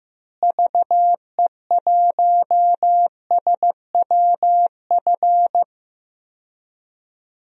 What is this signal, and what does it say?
Morse code, heard as a keyed tone, about 15 wpm, 700 Hz. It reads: VE1SWF
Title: DXCC-100 Call Signs - 6 Characters - Repeated 2x - 15wpm